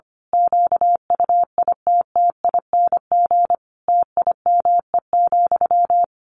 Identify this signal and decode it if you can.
Morse code, heard as a keyed tone, about 25 wpm, 700 Hz. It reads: QUITTING TIME,
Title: A Sound Of Thunder 25wpm